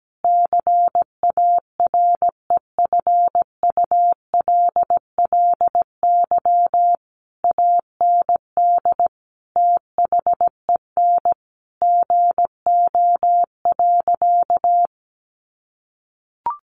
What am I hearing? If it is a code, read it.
Morse code, heard as a keyed tone, about 17 wpm, 700 Hz. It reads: CAREFULLY AND THEN GO.
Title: Sentences from Top 1300 Words 17wpm